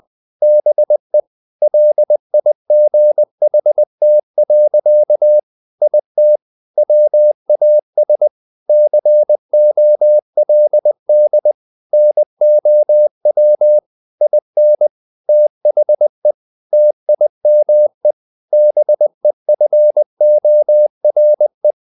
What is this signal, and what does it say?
Morse code, heard as a keyed tone, about 20 wpm, 600 Hz. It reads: BE LIGHT. IT WAS COLD NOW IN THE TIME BEFORE
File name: the old man and the sea / the old man and the sea part 1 20wpm